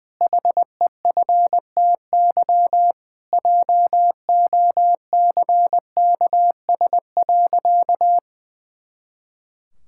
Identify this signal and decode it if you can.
Morse code, heard as a keyed tone, about 20 wpm, 700 Hz. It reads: HEFTY JOCKS.
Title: Pangram Challenge v2 - 20wpm